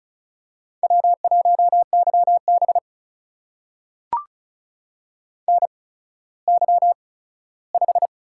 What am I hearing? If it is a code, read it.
Morse code, heard as a keyed tone, about 36 wpm, spaced as for 17 wpm, 700 Hz. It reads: W1YB NY5
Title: USA Call Signs - 4 Characters - 8x Character Spacing - 35wpm